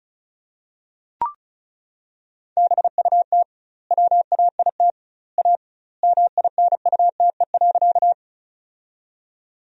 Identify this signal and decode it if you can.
Morse code, heard as a keyed tone, about 34 wpm, 700 Hz. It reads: BUT WAIT A MINUTE.
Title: 2x Word-Spacing - Sentences from Top 500 Words - 35wpm